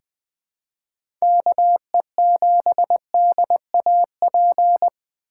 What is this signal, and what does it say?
Morse code, heard as a keyed tone, about 20 wpm, 700 Hz. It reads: KE7DAP
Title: USA Call Signs - 6 Characters - 2x Character Spacing - 20wpm